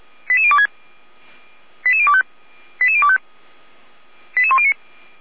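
Five-tone selective call (ZVEI), heard as quick runs of five tones, tones of about 70 ms. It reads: 88026 88025 88025 88109
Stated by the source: ZVEI five-tone call received on an SDR.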